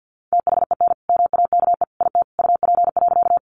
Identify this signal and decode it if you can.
Morse code, heard as a keyed tone, about 50 wpm, 700 Hz. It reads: THEN MAKE IT UP.
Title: Sentences from Top 100 Words 50wpm